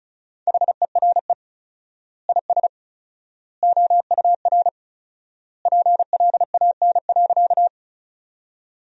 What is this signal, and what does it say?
Morse code, heard as a keyed tone, about 34 wpm, 700 Hz. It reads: HERE IS OUR PLAN.
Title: 4x Word-Spacing - Sentences from Top 500 Words - 35wpm